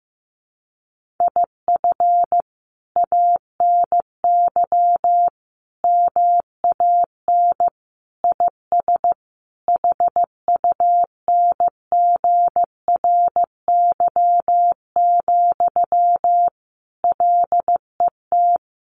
Morse code, heard as a keyed tone, about 15 wpm, 700 Hz. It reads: IF ANY MAN IS HUNGRY, LET
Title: Sentences from Top 1400 Words 15wpm